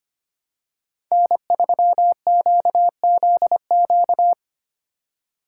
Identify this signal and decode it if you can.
Morse code, heard as a keyed tone, about 25 wpm, 700 Hz. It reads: N3QZQ
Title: USA Call Signs - 4x Character Spacing - 25wpm